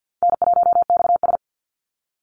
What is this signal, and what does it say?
Morse code, heard as a keyed tone, about 50 wpm, 700 Hz. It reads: N1XS
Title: Easy Call Signs Repeated 2x - 50wpm